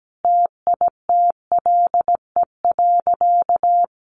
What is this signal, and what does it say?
Morse code, heard as a keyed tone, about 17 wpm, 700 Hz. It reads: TITLE.
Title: Sentences from Top 1300 Words 17wpm